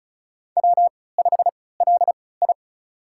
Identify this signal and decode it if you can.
Morse code, heard as a keyed tone, about 36 wpm, spaced as for 20 wpm, 690 Hz. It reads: W5LI
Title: USA Call Signs - 4 Characters - 3x Character Spacing - 35wpm